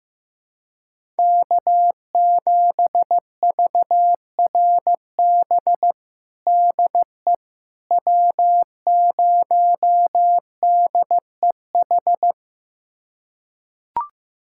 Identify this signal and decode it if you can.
Morse code, heard as a keyed tone, about 16 wpm, 700 Hz. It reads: K7VRB DE W0DEH
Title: USA Callsign DE Callsign - 5 Characters - 15wpm